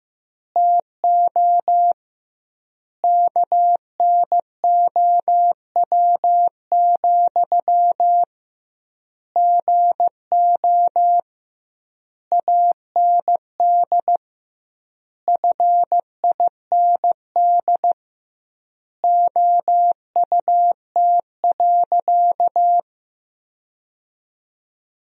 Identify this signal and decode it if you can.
Morse code, heard as a keyed tone, about 15 wpm, 700 Hz. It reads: TO KNOW, GO AND FIND OUT.
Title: 2x Word-Spacing - Sentences from Top 200 Words - 15wpm